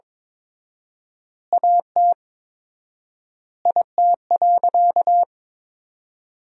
Morse code, heard as a keyed tone, about 21 wpm, 700 Hz. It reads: AT IT.
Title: 4x Word-Spacing - Sentences from Top 100 Words - 22wpm